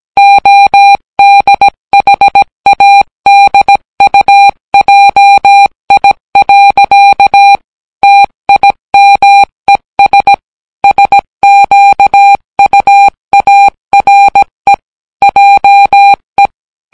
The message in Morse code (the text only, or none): ODHADUJI. TIMES SQUARE JE